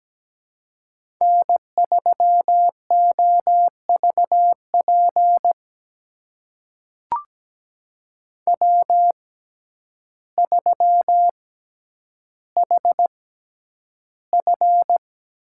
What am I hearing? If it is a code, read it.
Morse code, heard as a keyed tone, about 17 wpm, spaced as for 9 wpm, 700 Hz. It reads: N3OVP W3HF
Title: USA Call Signs - 6x Character Spacing - 17wpm